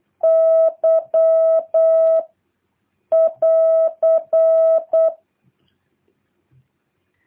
Morse code, heard as a keyed tone, about 8 wpm, 650 Hz. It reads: Y <AR>
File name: jh8cbh / 2023cw